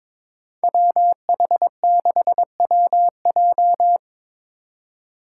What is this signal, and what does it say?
Morse code, heard as a keyed tone, about 22 wpm, 700 Hz. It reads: WH6WJ
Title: USA Call Signs - 5 Characters - 3x Character Spacing - 22wpm